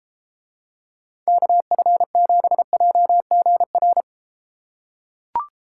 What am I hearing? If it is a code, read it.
Morse code, heard as a keyed tone, about 32 wpm, 700 Hz. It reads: KF7JGR